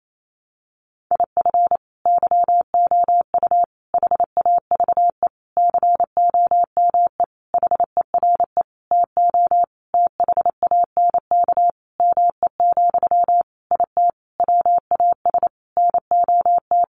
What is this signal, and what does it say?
Morse code, heard as a keyed tone, about 28 wpm, 700 Hz. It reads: IF YOU HAVE COME HERE TO THANK ME, IT WAS NOT
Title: Sentences from Top 800 Words 28wpm